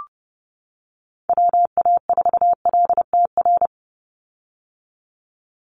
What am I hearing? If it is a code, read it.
Morse code, heard as a keyed tone, about 30 wpm, 700 Hz. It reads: WA4LTR
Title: USA Call Signs 30wpm